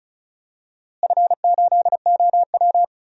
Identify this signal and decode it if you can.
Morse code, heard as a keyed tone, about 34 wpm, 700 Hz. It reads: F8OW
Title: DXCC-100 Call Signs - 4 Characters - Repeated 3x - 35wpm